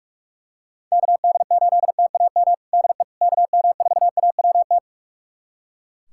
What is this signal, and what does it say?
Morse code, heard as a keyed tone, about 42 wpm, 700 Hz. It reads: KD8TAM DE KM4AWT
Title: USA Callsign DE Callsign - 6 Characters - 45wpm